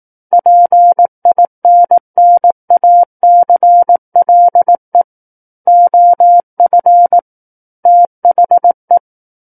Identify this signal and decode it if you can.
Morse code, heard as a keyed tone, about 18 wpm, 700 Hz. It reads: PINNACLE OF THE
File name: Book One / 09-Chapter 9 - 18 WPM